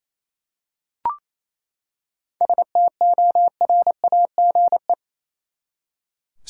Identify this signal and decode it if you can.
Morse code, heard as a keyed tone, about 27 wpm, 700 Hz. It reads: STORAGE
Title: Seven Letter Words - 28wpm